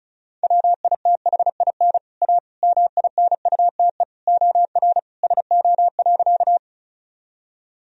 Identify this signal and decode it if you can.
Morse code, heard as a keyed tone, about 36 wpm, 700 Hz. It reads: WITHIN A MINUTE OR SO.
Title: Sentences from Top 1000 Words - Review Part 4 - 35wpm